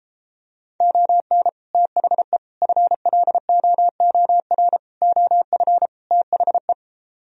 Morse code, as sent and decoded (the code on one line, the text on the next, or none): --- -. / - .... / ..-. .-.. --- --- .-. / --- ..-. / - .... .
ON THE FLOOR OF THE